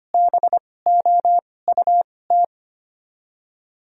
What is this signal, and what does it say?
Morse code, heard as a keyed tone, about 25 wpm, spaced as for 18 wpm, 700 Hz. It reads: BOUT
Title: 2x Character Spacing - Top 100 Words - 25wpm